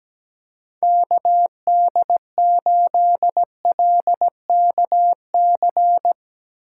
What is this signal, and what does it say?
Morse code, heard as a keyed tone, about 17 wpm, 700 Hz. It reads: KD8LKC
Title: USA Call Signs - 2x Character Spacing - 17wpm